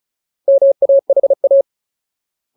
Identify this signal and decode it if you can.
Morse code, heard as a keyed tone, about 34 wpm, 550 Hz. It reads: MAHA
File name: Ultimate Word List / 35WPM Ultimate